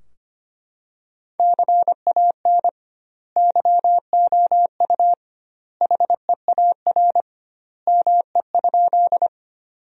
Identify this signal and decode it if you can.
Morse code, heard as a keyed tone, about 25 wpm, 700 Hz. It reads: CAN YOU HEAR ME?
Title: 2x Word-Spacing - Sentences from Top 300 Words - 25wpm